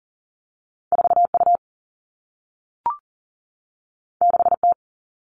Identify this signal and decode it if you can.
Morse code, heard as a keyed tone, about 40 wpm, 700 Hz. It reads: 4U 6T
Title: All Cut Numbers - 2-Digit 40wpm